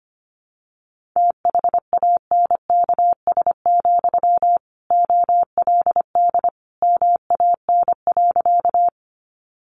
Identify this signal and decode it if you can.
Morse code, heard as a keyed tone, about 25 wpm, 700 Hz. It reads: THANKS, OLD MAN.
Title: Sentences from Top 600 Words 25wpm